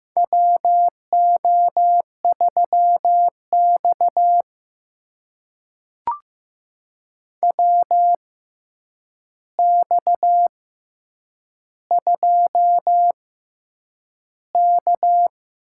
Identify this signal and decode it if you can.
Morse code, heard as a keyed tone, about 15 wpm, spaced as for 8 wpm, 700 Hz. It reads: WO3X WX2K
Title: USA Call Signs - 4 Characters - 6x Character Spacing - 15wpm